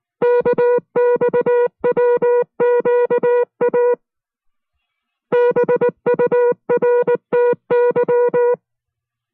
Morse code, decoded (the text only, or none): KXWQA BURTY